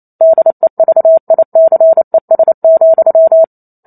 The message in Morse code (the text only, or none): DEVICES,